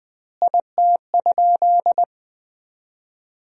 Morse code, heard as a keyed tone, about 21 wpm, 700 Hz. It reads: IT?
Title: 2x Word-Spacing - Sentences from Top 100 Words - 20wpm